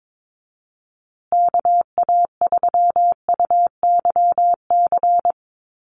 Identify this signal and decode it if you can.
Morse code, heard as a keyed tone, about 22 wpm, 700 Hz. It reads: KA3UYC
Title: USA Call Signs Repeated 2x 22wpm